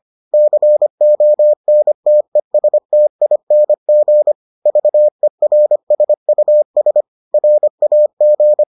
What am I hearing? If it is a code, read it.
Morse code, heard as a keyed tone, about 25 wpm, 600 Hz. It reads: CONTESTING VERSUS RAG